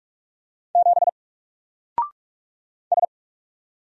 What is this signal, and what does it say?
Morse code, heard as a keyed tone, about 44 wpm, 700 Hz. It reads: 7 S
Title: Single Letter-Number Alt 45wpm